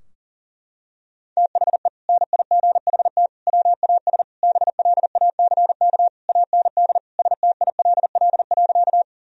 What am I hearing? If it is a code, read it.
Morse code, heard as a keyed tone, about 40 wpm, 700 Hz. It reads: THE NIGHT WAS BLACK AND STILL.
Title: Sentences from Top 1000 Words - Review Part 2 - 40wpm